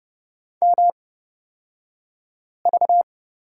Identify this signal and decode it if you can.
Morse code, heard as a keyed tone, about 30 wpm, spaced as for 5 wpm, 700 Hz. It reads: MV
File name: Lesson 38d - Letter q - Two Characters 30wpm